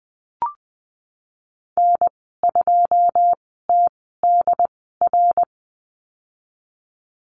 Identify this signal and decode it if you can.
Morse code, heard as a keyed tone, about 20 wpm, spaced as for 15 wpm, 700 Hz. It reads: N2TDR
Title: 186 Lesson 30g - Letter b - Call Signs 20wpm